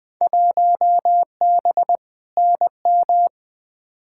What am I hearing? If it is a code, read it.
Morse code, heard as a keyed tone, about 20 wpm, 700 Hz. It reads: 1B NM
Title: Field Day Exchanges Encore - 20wpm